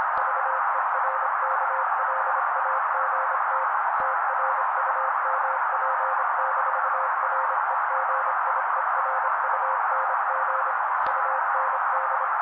Morse code, heard as a keyed tone, about 26 wpm, 530 Hz. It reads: VERKLAGT TRUMP-REGIERUNG UND
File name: daily / 25wpm noise